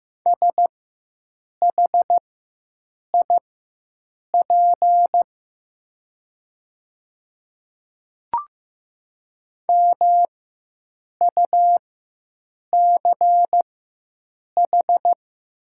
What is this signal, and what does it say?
Morse code, heard as a keyed tone, about 15 wpm, spaced as for 7 wpm, 700 Hz. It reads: SHIP MUCH
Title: 4x Character Spacing - Top 500 Words - 15wpm